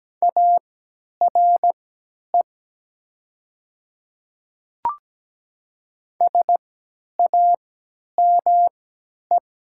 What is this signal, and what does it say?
Morse code, heard as a keyed tone, about 17 wpm, spaced as for 10 wpm, 700 Hz. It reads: ARE SAME
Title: 3x Character Spacing - Top 300 Words - 17wpm